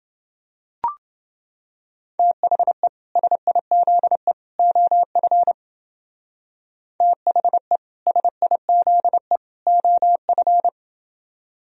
Morse code, heard as a keyed tone, about 29 wpm, 700 Hz. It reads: THE SIZE OF THE SIZE OF